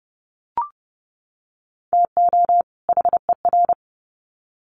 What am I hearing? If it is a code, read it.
Morse code, heard as a keyed tone, about 30 wpm, 700 Hz. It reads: TO HER
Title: Sets of 2 Words 30wpm